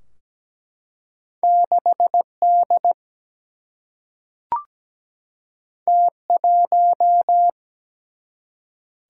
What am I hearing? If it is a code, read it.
Morse code, heard as a keyed tone, about 17 wpm, 700 Hz. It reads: 6 D T 1
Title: Single-Letter Single-Number 17wpm